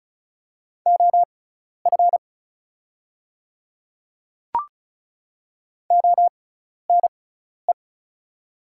Morse code, heard as a keyed tone, about 36 wpm, spaced as for 12 wpm, 690 Hz. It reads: OF ONE
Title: 6x Character Spacing - Top 300 Words - 35wpm